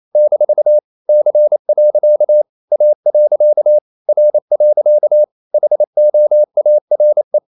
Morse code, heard as a keyed tone, about 28 wpm, 600 Hz. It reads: - C. A. R. HOARE
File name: CW / Practice / EnglishQuotations0228